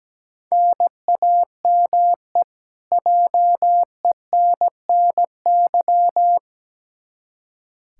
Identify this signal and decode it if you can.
Morse code, heard as a keyed tone, about 17 wpm, 700 Hz. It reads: NAME JENNY
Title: K1USN SST QSO Practice 17wpm